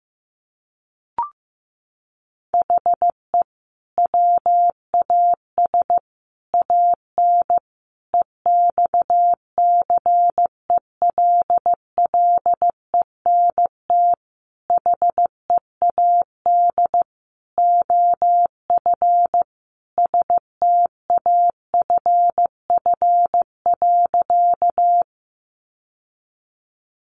Morse code, heard as a keyed tone, about 15 wpm, 700 Hz. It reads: HE WAS AN EXCELLENT HEAD OF STAFF.
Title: Sentences from Top 1200 Words 15wpm